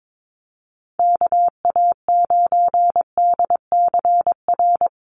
Morse code, heard as a keyed tone, about 22 wpm, 700 Hz. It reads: KA9DCR
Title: USA Call Signs Repeated 2x 22wpm